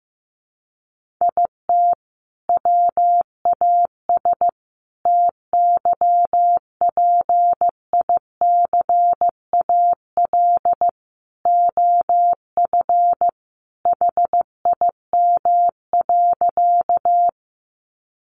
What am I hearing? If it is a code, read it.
Morse code, heard as a keyed tone, about 15 wpm, 700 Hz. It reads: IT WAS TYPICAL OF HIM.
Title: Sentences from Top 1100 Words 15wpm